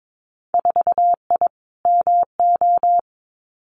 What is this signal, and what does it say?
Morse code, heard as a keyed tone, about 22 wpm, 700 Hz. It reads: 4I MO
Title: Winter Field Day Exchanges 22wpm